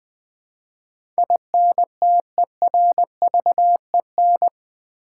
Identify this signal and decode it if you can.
Morse code, heard as a keyed tone, about 20 wpm, 700 Hz. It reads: INTERVEN
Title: Build A Word - 4 Letters At A Time - 12-Letter Words - 20wpm